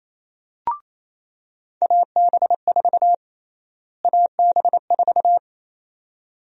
Morse code, heard as a keyed tone, about 28 wpm, 700 Hz. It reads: AB4 AB4
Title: USA Call Sign Prefixes - Repeated 2x - 28wpm